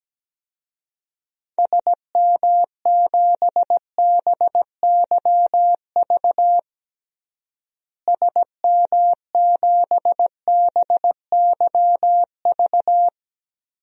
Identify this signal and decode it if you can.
Morse code, heard as a keyed tone, about 17 wpm, 700 Hz. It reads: SM7BYV SM7BYV
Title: DXCC-100 Call Signs - 6 Characters - Repeated 3x - 17wpm